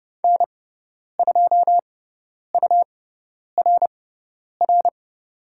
Morse code, heard as a keyed tone, about 30 wpm, spaced as for 10 wpm, 690 Hz. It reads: N2URR
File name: Lesson 18g - Letter m - Call Signs 30wpm